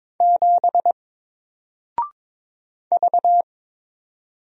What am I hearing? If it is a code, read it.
Morse code, heard as a keyed tone, about 22 wpm, 700 Hz. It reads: 7 V
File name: Single-Letter-Number-22wpm